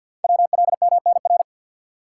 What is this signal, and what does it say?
Morse code, heard as a keyed tone, about 50 wpm, 700 Hz. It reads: WLMNR
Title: Code Groups - 5 Letters - 50wpm